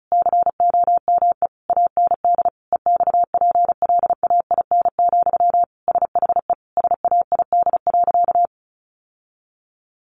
Morse code, heard as a keyed tone, about 34 wpm, 700 Hz. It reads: COME AND EXPLAIN, SHE SAID.
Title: Sentences from Top 1100 Words 35wpm